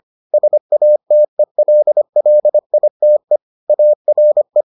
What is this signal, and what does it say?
Morse code, heard as a keyed tone, about 25 wpm, 600 Hz. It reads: SATELLITE ARE